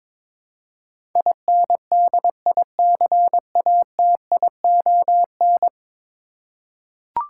Morse code, heard as a keyed tone, about 22 wpm, 700 Hz. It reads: INDICATION
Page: morse-code-ninja-practice-22wpm